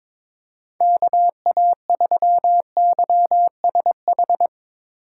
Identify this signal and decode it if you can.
Morse code, heard as a keyed tone, about 22 wpm, 700 Hz. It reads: KA3YSH
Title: USA Call Signs - 6 Characters - 6x Character Spacing - 22wpm